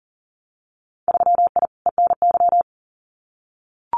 Morse code, heard as a keyed tone, about 40 wpm, 700 Hz. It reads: 3I ENY
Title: Winter Field Day Exchanges Encore 40wpm